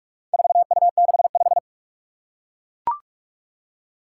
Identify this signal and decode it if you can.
Morse code, heard as a keyed tone, about 41 wpm, 700 Hz. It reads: 4U65
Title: Four Challenging Letter-Number Combinations 45wpm